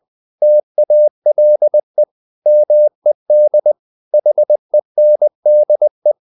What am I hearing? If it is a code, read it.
Morse code, heard as a keyed tone, about 20 wpm, 600 Hz. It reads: TALE MED HENDE
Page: Ferie 20 wpm